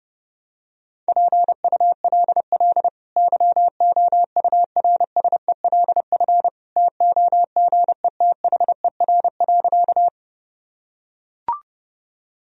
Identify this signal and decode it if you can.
Morse code, heard as a keyed tone, about 32 wpm, 700 Hz. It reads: PULL YOURSELF TOGETHER.
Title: Sentences from Top 1000 Words - Review Part 2 - 30wpm